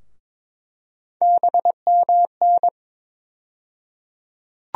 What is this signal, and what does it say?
Morse code, heard as a keyed tone, about 22 wpm, 700 Hz. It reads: BMN
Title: USA Call Sign Postfixes - 3 Letters 22wpm